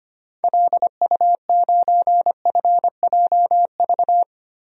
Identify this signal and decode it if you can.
Morse code, heard as a keyed tone, about 25 wpm, 700 Hz. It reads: LU9FJV
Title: DXCC-100 Call Signs - 6 Characters - Repeated 3x - 25wpm